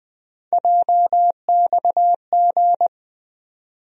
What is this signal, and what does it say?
Morse code, heard as a keyed tone, about 20 wpm, 700 Hz. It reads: JXG
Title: USA Call Sign Postfixes - 3 Letters 20wpm